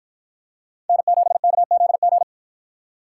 Morse code, heard as a keyed tone, about 50 wpm, 700 Hz. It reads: N7KZG